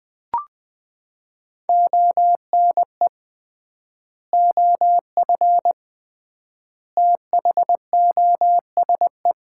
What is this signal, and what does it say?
Morse code, heard as a keyed tone, about 20 wpm, 700 Hz. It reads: ONE OF THOSE
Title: 3x Word-Spacing - Sets of 3 Words - 20wpm